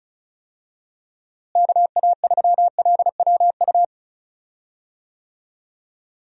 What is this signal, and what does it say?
Morse code, heard as a keyed tone, about 34 wpm, 700 Hz. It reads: KA3LWU
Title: USA Call Signs - 6 Characters - Repeated 2x - 35wpm